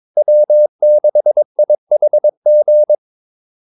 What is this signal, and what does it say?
Morse code, heard as a keyed tone, about 22 wpm, 600 Hz. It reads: W6IHG
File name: Callsigns-22wpm